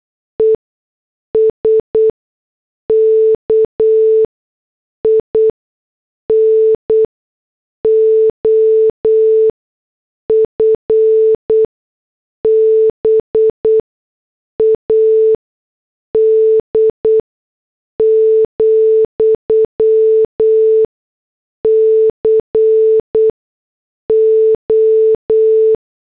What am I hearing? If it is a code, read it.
Morse code, heard as a keyed tone, about 8 wpm, spaced as for 6 wpm, 440 Hz. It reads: ESKIN OF BAD, CO